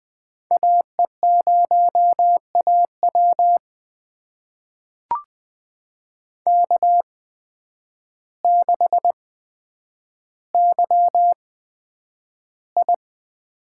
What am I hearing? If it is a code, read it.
Morse code, heard as a keyed tone, about 20 wpm, spaced as for 9 wpm, 700 Hz. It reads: AE0AW K6YI
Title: USA Call Signs - 8x Character Spacing - 20wpm